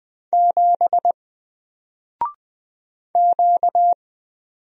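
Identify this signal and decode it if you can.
Morse code, heard as a keyed tone, about 20 wpm, 700 Hz. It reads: 7 Q
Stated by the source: Single letter and number Morse drill at 20 wpm, from N4LCD.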